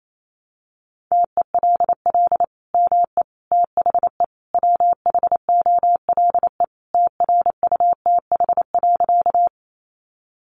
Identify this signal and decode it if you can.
Morse code, heard as a keyed tone, about 28 wpm, 700 Hz. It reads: TELL ME THE WHOLE TRUTH.
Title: Sentences from Top 800 Words 28wpm